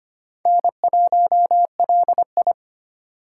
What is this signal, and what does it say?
Morse code, heard as a keyed tone, about 25 wpm, 700 Hz. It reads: N1LI
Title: USA Call Signs - 4 Characters - 2x Character Spacing - 25wpm